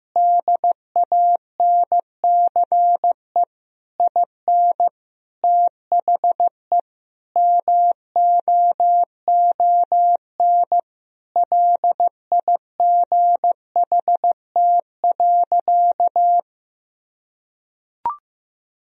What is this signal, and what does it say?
Morse code, heard as a keyed tone, about 16 wpm, 700 Hz. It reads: DANCE IN THE MOON LIGHT.
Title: Sentences from Top 1000 Words - Review Part 4 - 15wpm